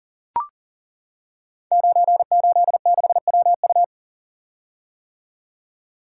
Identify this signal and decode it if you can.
Morse code, heard as a keyed tone, about 42 wpm, 690 Hz. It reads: 986WU
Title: Code Groups - 5 Letter:Numbers - 40wpm